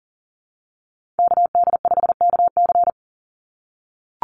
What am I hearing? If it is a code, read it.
Morse code, heard as a keyed tone, about 38 wpm, 700 Hz. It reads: KD5KC
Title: NA SOTA Activator Callsigns 40wpm